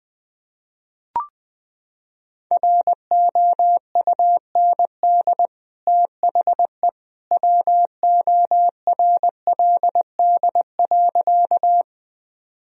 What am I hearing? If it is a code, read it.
Morse code, heard as a keyed tone, about 20 wpm, 700 Hz. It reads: ROUND THE WORLD.
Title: Sentences from Top 1000 Words - Review Part 1 - 20wpm